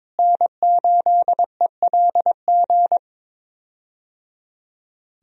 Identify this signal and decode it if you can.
Morse code, heard as a keyed tone, about 22 wpm, 700 Hz. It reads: N8ELG
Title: USA Call Signs - 5 Characters - Repeated 3x - 22wpm